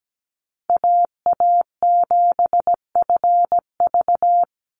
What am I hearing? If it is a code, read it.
Morse code, heard as a keyed tone, about 17 wpm, 700 Hz. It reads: AA7FV
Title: Call Signs Repeated 3x - 17wpm